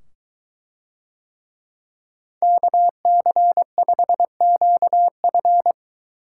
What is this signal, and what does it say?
Morse code, heard as a keyed tone, about 22 wpm, 700 Hz. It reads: KC5QF